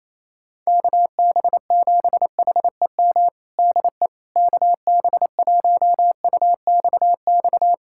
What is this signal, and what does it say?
Morse code, heard as a keyed tone, about 28 wpm, 700 Hz. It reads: KB7HEM DE KB1UXX
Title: USA Callsign DE Callsign - 6 Characters - 28wpm